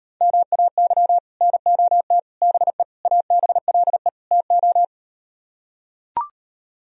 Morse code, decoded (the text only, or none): MAY NOT BE ABLE TO